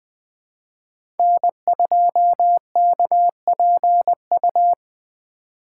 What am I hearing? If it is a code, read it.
Morse code, heard as a keyed tone, about 21 wpm, 700 Hz. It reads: N2KPU
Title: USA Call Signs - 5 Characters - 3x Character Spacing - 20wpm